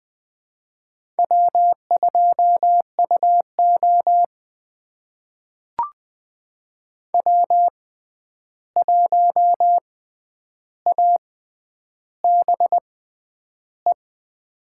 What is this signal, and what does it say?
Morse code, heard as a keyed tone, about 20 wpm, spaced as for 10 wpm, 700 Hz. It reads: W2UO W1ABE
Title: USA Call Signs - 6x Character Spacing - 20wpm